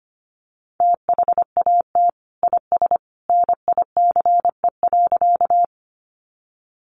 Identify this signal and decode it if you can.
Morse code, heard as a keyed tone, about 25 wpm, 700 Hz. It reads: THAT IS NICE.
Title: Sentences from Top 1100 Words 25wpm